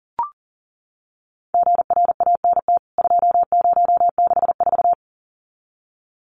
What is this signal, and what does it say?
Morse code, heard as a keyed tone, about 40 wpm, 700 Hz. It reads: GRANT 2064
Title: CWT Exchange 40wpm